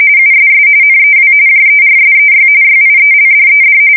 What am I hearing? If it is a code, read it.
RTTY, about 45.45 baud. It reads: USS NEW JERSEY YIELDED
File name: fsk170 / USS-New-Jersey